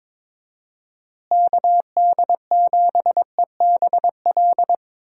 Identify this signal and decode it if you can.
Morse code, heard as a keyed tone, about 22 wpm, 700 Hz. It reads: KD7EBL
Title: USA Call Signs - 6 Characters - Repeated 3x - 22wpm